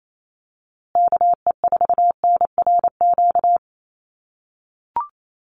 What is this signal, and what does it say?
Morse code, heard as a keyed tone, about 28 wpm, 700 Hz. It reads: KE4NRQ
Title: USA Call Signs Repeated 2x 28wpm